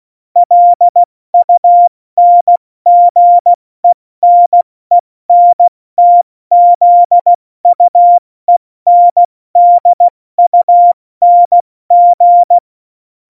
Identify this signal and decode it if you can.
Morse code, heard as a keyed tone, about 16 wpm, 700 Hz. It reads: LUNGENENTZUENDUNG
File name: Morsekurs / Witze / Witz 12. Dezember 2017, 70 F80